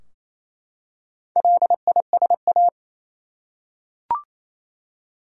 Morse code, heard as a keyed tone, about 27 wpm, 700 Hz. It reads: LISA
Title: USA Common Names - Max 4 Letters - 28wpm